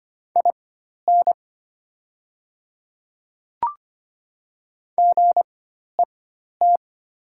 Morse code, heard as a keyed tone, about 25 wpm, spaced as for 12 wpm, 690 Hz. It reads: IN GET
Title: 4x Character Spacing - Top 200 Words - 25wpm